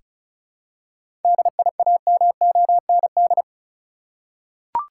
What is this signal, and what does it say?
Morse code, heard as a keyed tone, about 36 wpm, 700 Hz. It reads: DIAMOND
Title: Seven Letter Words - 35wpm